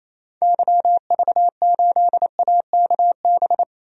Morse code, heard as a keyed tone, about 28 wpm, 700 Hz. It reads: YV8AKB
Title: DXCC-100 Call Signs - 6 Characters - Repeated 3x - 28wpm